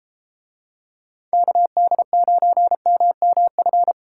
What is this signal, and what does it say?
Morse code, heard as a keyed tone, about 30 wpm, 700 Hz. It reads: KD9MMF